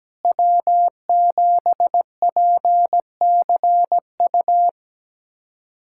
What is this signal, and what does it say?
Morse code, heard as a keyed tone, about 17 wpm, 700 Hz. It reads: W7PCU
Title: USA Callsign DE Callsign - 5 Characters - 17wpm